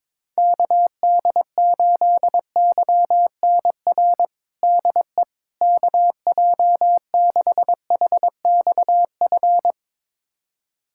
Morse code, heard as a keyed tone, about 22 wpm, 700 Hz. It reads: KD8YNR DE KJ6HXF
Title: USA Callsign DE Callsign 22wpm